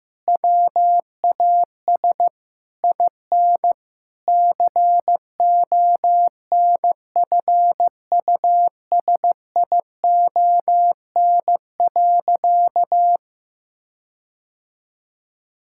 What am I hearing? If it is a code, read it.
Morse code, heard as a keyed tone, about 16 wpm, 700 Hz. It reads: WAS IN CONFUSION.
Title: Sentences from Top 1600 Words 15wpm